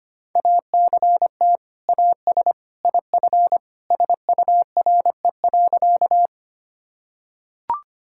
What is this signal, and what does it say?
Morse code, heard as a keyed tone, about 25 wpm, 700 Hz. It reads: ACT AS IF SURE.
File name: practice / 25wpm / Sentences with 2-Word Phrases - 25wpm